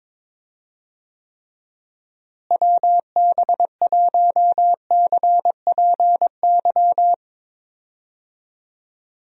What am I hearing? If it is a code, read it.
Morse code, heard as a keyed tone, about 22 wpm, 700 Hz. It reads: WB1CPY
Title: USA Call Signs - 6 Characters - Repeated 2x - 22wpm